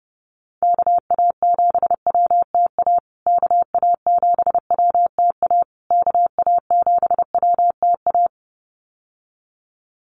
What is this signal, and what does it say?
Morse code, heard as a keyed tone, about 30 wpm, 700 Hz. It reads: KA7WTA KA7WTA KA7WTA
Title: USA Call Signs Repeated 3x 30wpm